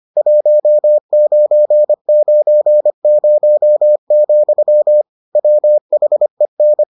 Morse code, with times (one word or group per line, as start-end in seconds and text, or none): 0.2-5.0 1990,
5.4-6.8 WHEN